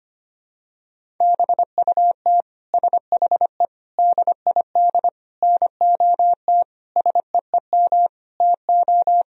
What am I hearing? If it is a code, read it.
Morse code, heard as a keyed tone, about 25 wpm, 700 Hz. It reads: BUT SHE DID NOT SEEM TO
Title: Sentences from Top 1000 Words - Review Part 1 - 25wpm